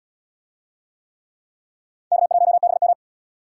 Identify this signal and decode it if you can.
Morse code, heard as a keyed tone, about 68 wpm, 700 Hz. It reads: K1BM